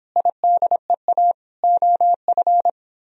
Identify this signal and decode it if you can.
Morse code, heard as a keyed tone, about 25 wpm, 700 Hz. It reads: IDEA OF
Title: Speed-Racing - Sets of 3 Words 22wpm